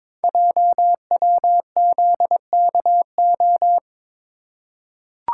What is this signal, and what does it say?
Morse code, heard as a keyed tone, about 23 wpm, 700 Hz. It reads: JWZKO
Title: Code Groups - 5 Letters - 22wpm